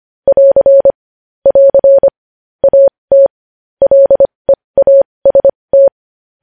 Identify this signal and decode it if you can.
Morse code, heard as a keyed tone, about 25 wpm, spaced as for 20 wpm, 550 Hz. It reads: <AR> <AR> AT LEAST
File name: podcast / News headlines-1x-25x20